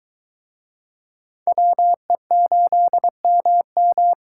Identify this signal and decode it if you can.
Morse code, heard as a keyed tone, about 24 wpm, 700 Hz. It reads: WE8MM